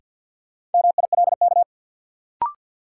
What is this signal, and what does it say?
Morse code, heard as a keyed tone, about 52 wpm, 700 Hz. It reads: MILK
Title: Sentences from Top 1500 Words - New Word Drill Alt - 50wpm